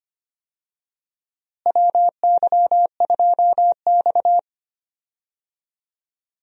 Morse code, heard as a keyed tone, about 25 wpm, 700 Hz. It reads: WY2X
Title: USA Call Signs - 4 Characters - Repeated 2x - 25wpm